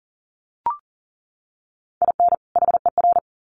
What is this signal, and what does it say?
Morse code, heard as a keyed tone, about 40 wpm, 700 Hz. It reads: IN HER
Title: Sets of 2 Words 40wpm